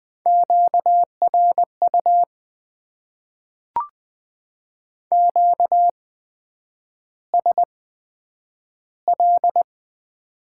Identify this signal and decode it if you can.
Morse code, heard as a keyed tone, about 20 wpm, spaced as for 9 wpm, 700 Hz. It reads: QRU QSL
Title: Top 90 QSO Elements - 8x Character Spacing - 20wpm